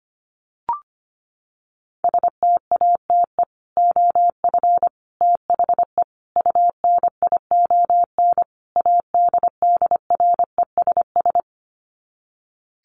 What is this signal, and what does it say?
Morse code, heard as a keyed tone, about 25 wpm, 700 Hz. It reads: STATE OF THE UNION ADDRESS
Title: Sets of 5 Words Encore 25wpm